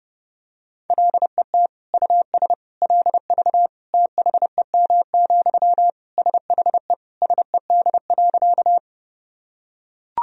Morse code, heard as a keyed tone, about 30 wpm, 700 Hz. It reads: LET US LV THEM, SHE SED.
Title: Sentences from Top 400 Words and Abbreviations 30wpm